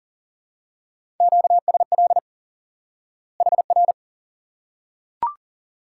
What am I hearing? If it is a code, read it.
Morse code, heard as a keyed tone, about 39 wpm, 700 Hz. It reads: QSL HR
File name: IQE Course 3 - Lesson 89 - YAGI - Two Elements - 40wpm